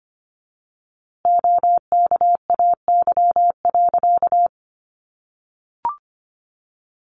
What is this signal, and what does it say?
Morse code, heard as a keyed tone, about 25 wpm, 700 Hz. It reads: OKAY.
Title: A Sound Of Thunder 25wpm